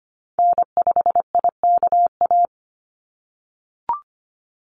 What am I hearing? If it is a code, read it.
Morse code, heard as a keyed tone, about 25 wpm, 700 Hz. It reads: N5IKA
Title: USA Call Signs 25wpm